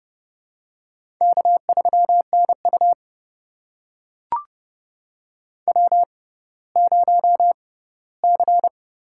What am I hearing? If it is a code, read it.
Morse code, heard as a keyed tone, about 32 wpm, spaced as for 17 wpm, 700 Hz. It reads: K3NU W0C
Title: USA Call Signs - 4 Characters - 6x Character Spacing - 30wpm